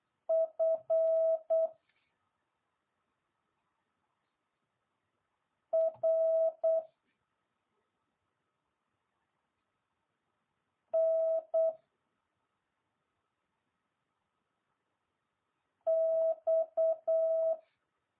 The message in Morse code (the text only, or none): F R N X